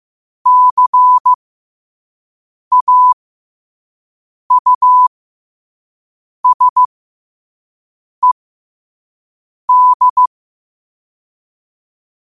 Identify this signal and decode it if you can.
Morse code, heard as a keyed tone, about 15 wpm, spaced as for 5 wpm, 1000 Hz. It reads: CAUSED
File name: qotdfiles / 5wpm